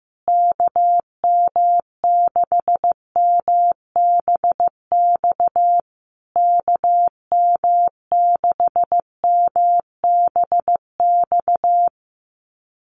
Morse code, heard as a keyed tone, about 15 wpm, 700 Hz. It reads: KM6MBX KM6MBX
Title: USA Call Signs Repeated 3x 15wpm